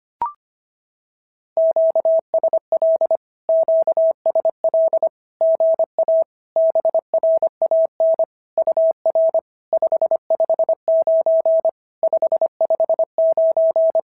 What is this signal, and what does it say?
Morse code, heard as a keyed tone, about 25 wpm, 650 Hz. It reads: QSL QSL GA BRAN UR 559 559
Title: SOTA QSO Practice 25wpm